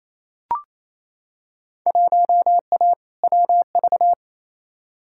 Morse code, heard as a keyed tone, about 28 wpm, 700 Hz. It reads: 1A WV
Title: Field Day Exchanges Encore - 28wpm